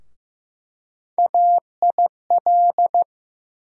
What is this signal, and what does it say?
Morse code, heard as a keyed tone, about 15 wpm, 700 Hz. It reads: AIL
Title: USA Call Sign Postfixes - 3 Letters 15wpm